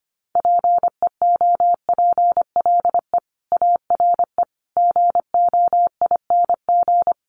Morse code, heard as a keyed tone, about 25 wpm, 700 Hz. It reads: PEOPLE ARE GOING
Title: Sets of 4 Words 25wpm